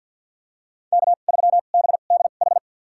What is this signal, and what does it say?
Morse code, heard as a keyed tone, about 53 wpm, spaced as for 37 wpm, 700 Hz. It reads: K3BDH
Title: USA Call Signs - 2x Character Spacing - 50wpm